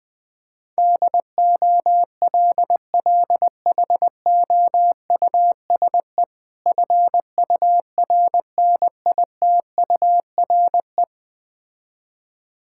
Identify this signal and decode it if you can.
Morse code, heard as a keyed tone, about 20 wpm, 700 Hz. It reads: DOLLHOUSE FURNITURE
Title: Difficult Words with Hints - 20wpm